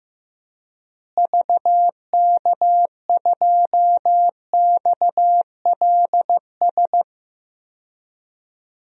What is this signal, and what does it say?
Morse code, heard as a keyed tone, about 16 wpm, 700 Hz. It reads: VK2XLS
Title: DXCC-100 Call Signs - 6 Characters - Repeated 3x - 15wpm